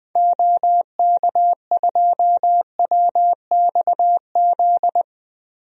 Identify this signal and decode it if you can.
Morse code, heard as a keyed tone, about 20 wpm, 700 Hz. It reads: OK2WXZ
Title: DXCC-100 Call Signs - 6 Characters - 20wpm